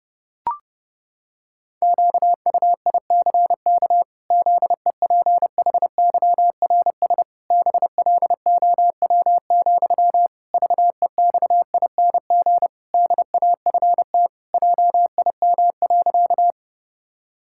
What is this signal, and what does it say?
Morse code, heard as a keyed tone, about 29 wpm, 700 Hz. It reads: QUICK ZEPHYRS BLOW, VEXING DAFT JIM.
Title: Pangram Challenge v2 - 30wpm